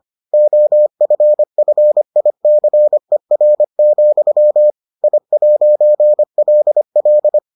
Morse code, heard as a keyed tone, about 25 wpm, 600 Hz. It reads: OFFICER, I'LL